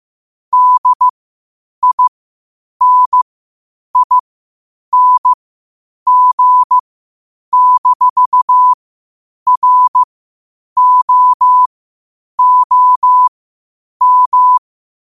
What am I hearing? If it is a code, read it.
Morse code, heard as a keyed tone, about 15 wpm, spaced as for 8 wpm, 1000 Hz. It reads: DINING-ROOM
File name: qotdfiles / 8wpm